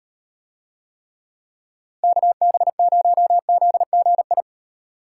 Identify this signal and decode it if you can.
Morse code, heard as a keyed tone, about 38 wpm, 700 Hz. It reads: KB0ZGI